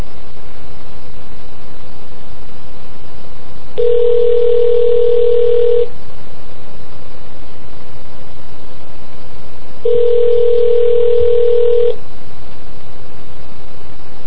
Ringback tone (North America), 440 + 480 Hz.